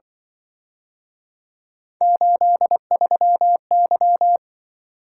Morse code, heard as a keyed tone, about 23 wpm, 700 Hz. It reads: 83Y
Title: Speed-Racing - Three Letter-Number Combinations - 20wpm